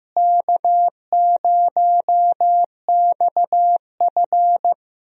Morse code, heard as a keyed tone, about 15 wpm, 700 Hz. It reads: K0XF
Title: USA Call Signs - 4 Characters - Repeated 2x - 15wpm